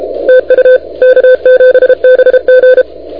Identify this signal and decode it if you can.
Morse code, heard as a keyed tone, about 29 wpm, 510 Hz. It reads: TU K7BG
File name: recordings / K7BG de NT1K ARRL10M CW 28MHz